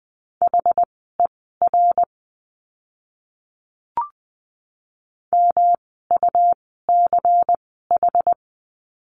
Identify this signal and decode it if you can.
Morse code, heard as a keyed tone, about 20 wpm, spaced as for 15 wpm, 700 Hz. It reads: HER MUCH
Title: 115 Lesson 19f - Letter w - Words 20wpm